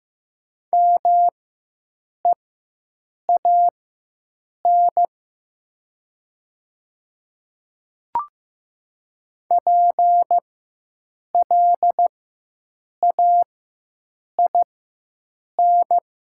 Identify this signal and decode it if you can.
Morse code, heard as a keyed tone, about 15 wpm, spaced as for 7 wpm, 700 Hz. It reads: MEAN PLAIN
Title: 4x Character Spacing - Top 500 Words - 15wpm